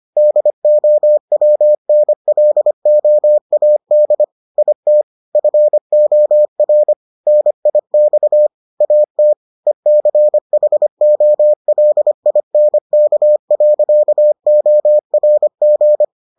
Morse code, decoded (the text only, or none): DOWNLOAD IT FOR NIX AT ECHOLINK.ORG